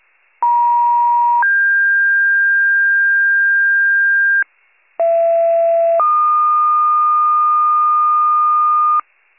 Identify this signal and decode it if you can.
Two-tone page, heard as a one-second tone, then a three-second tone, about 950 then 1600 Hz, about 670 then 1220 Hz, with noise. It is heard from 0.4 to 9.0 s.
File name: pager / 39 RESCUE / wcalert